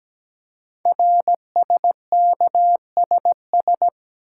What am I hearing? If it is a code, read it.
Morse code, heard as a keyed tone, about 17 wpm, 700 Hz. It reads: RSKSS
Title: Code Groups - 5 Letters - 17wpm